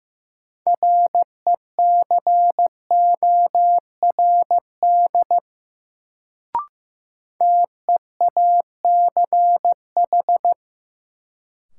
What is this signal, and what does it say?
Morse code, heard as a keyed tone, about 15 wpm, 700 Hz. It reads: RECORD TEACH
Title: Sentences from Top 500 Words - New Word Drill - 15wpm